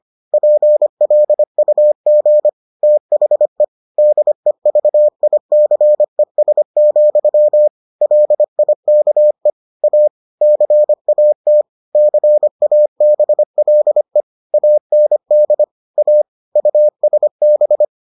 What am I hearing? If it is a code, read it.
Morse code, heard as a keyed tone, about 25 wpm, 600 Hz. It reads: PLUG THE DEVICES, LIKE A CAT CABLE AND A USB